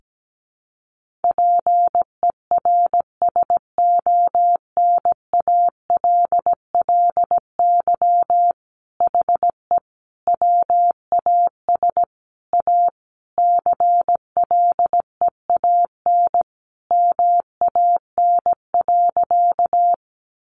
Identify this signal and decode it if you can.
Morse code, heard as a keyed tone, about 17 wpm, 700 Hz. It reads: PERSONALLY HE WAS A CLEAN MAN.
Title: Sentences from Top 1400 Words 17wpm